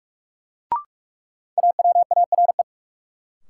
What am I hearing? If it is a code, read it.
Morse code, heard as a keyed tone, about 41 wpm, 700 Hz. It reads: AWARE
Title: Sentences from Top 1200 Words - New Word Drill Alt - 45wpm